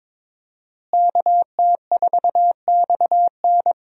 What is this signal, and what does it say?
Morse code, heard as a keyed tone, about 22 wpm, 700 Hz. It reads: KT4XN
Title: USA Call Signs - 5 Characters - Repeated 3x - 22wpm